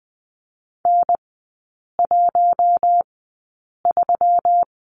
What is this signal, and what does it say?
Morse code, heard as a keyed tone, about 20 wpm, spaced as for 8 wpm, 700 Hz. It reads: N13
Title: 121 Lesson 20e - Number 3 - Three Characters 20wpm